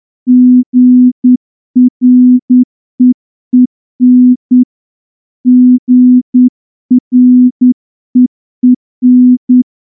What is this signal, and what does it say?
Morse code, heard as a keyed tone, about 9 wpm, 250 Hz. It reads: GREEN GREEN
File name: morse-training-10WPM